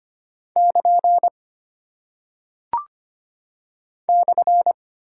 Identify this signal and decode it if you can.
Morse code, heard as a keyed tone, about 25 wpm, 700 Hz. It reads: <KN> /
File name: practice / 25wpm / Single Punctuation-Prosign - 25wpm